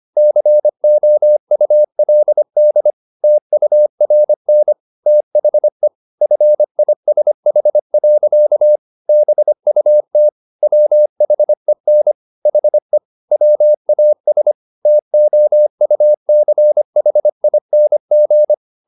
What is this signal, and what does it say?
Morse code, heard as a keyed tone, about 25 wpm, 600 Hz. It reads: COULD TURN THE FISH. BUT WHEN HE WAS TOUCHING
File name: the old man and the sea part 1 25wpm